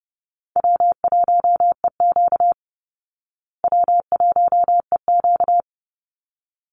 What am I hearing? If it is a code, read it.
Morse code, heard as a keyed tone, about 30 wpm, 700 Hz. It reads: W1EQ W1EQ
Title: Easy Call Signs Repeated 2x - 30wpm